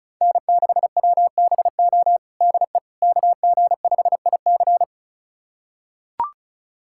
Morse code, heard as a keyed tone, about 36 wpm, 700 Hz. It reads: N6WBO DE KG5IC
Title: USA Callsign DE Callsign - 5 Characters - 35wpm